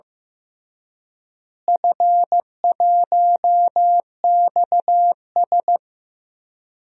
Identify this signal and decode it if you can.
Morse code, heard as a keyed tone, about 15 wpm, 700 Hz. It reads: F1XS
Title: DXCC-100 Call Signs - 4 Characters - Repeated 2x - 15wpm